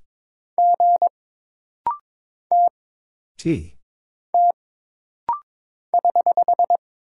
Morse code, heard as a keyed tone, about 22 wpm, 700 Hz. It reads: G T T <HH>
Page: This rapid-fire timing is geared to push your ICR (Instant Character Recognition) skills.